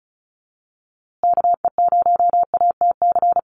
Morse code, heard as a keyed tone, about 34 wpm, 700 Hz. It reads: KE0ATC
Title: USA Call Signs Repeated 3x 35wpm